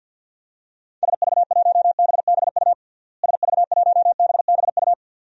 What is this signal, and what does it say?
Morse code, heard as a keyed tone, about 52 wpm, 700 Hz. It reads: SV1BBU SV1BBU
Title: DXCC-100 Call Signs - 6 Characters - Repeated 2x - 50wpm